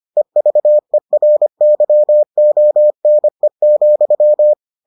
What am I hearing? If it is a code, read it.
Morse code, heard as a keyed tone, about 25 wpm, 600 Hz. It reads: EVERYONE,